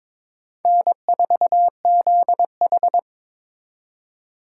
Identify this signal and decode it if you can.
Morse code, heard as a keyed tone, about 26 wpm, 700 Hz. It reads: N4ZH
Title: Speed Racing - USA Call Signs 22wpm